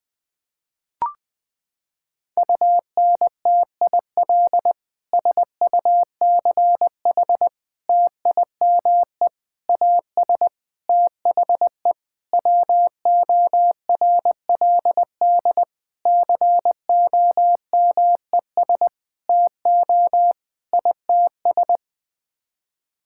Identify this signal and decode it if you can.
Morse code, heard as a keyed tone, about 20 wpm, 700 Hz. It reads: UNTIL SUCH TIME AS THE WORLD COMES TO ITS
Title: Sets of 9 Words - 20wpm